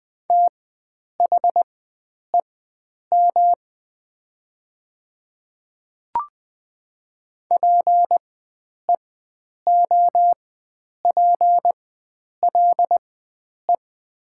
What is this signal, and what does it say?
Morse code, heard as a keyed tone, about 20 wpm, spaced as for 9 wpm, 700 Hz. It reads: THEM PEOPLE
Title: 4x Character Spacing - Top 300 Words - 20wpm